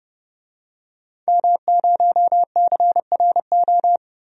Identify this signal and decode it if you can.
Morse code, heard as a keyed tone, about 31 wpm, 700 Hz. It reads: M0CRO
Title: DXCC-100 Call Signs - 5 Characters - Repeated 2x - 30wpm